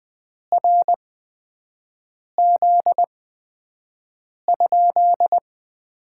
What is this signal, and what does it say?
Morse code, heard as a keyed tone, about 20 wpm, spaced as for 5 wpm, 700 Hz. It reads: RZ?
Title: Top 90 QSO Elements - 8x Character Spacing - 20wpm